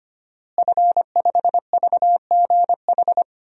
Morse code, heard as a keyed tone, about 25 wpm, 700 Hz. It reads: F5VGH
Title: DXCC-100 Call Signs - 5 Characters - Repeated 3x - 25wpm